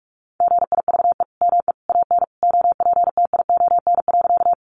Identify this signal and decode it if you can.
Morse code, heard as a keyed tone, about 46 wpm, 700 Hz. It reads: GIVE ME AN OPTION.
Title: Sentences from Top 1300 Words 45wpm